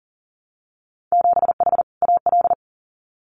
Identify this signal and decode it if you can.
Morse code, heard as a keyed tone, about 41 wpm, 700 Hz. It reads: 7H AL
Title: Winter Field Day Exchanges Encore 40wpm